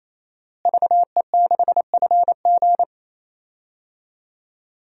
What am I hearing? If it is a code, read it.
Morse code, heard as a keyed tone, about 28 wpm, 700 Hz. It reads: VE6FG
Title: DXCC-100 Call Signs - 5 Characters - Repeated 3x - 28wpm